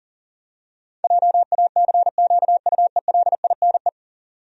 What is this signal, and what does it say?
Morse code, heard as a keyed tone, about 40 wpm, 700 Hz. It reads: JACQUELINE
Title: USA Common Names - 40wpm